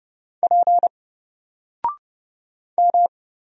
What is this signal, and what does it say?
Morse code, heard as a keyed tone, about 30 wpm, 700 Hz. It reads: P M
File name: Single-Letter-Number-30wpm